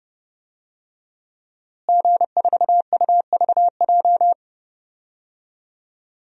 Morse code, heard as a keyed tone, about 30 wpm, 700 Hz. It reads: G4UVJ
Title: DXCC-100 Call Signs - 5 Characters - Repeated 3x - 30wpm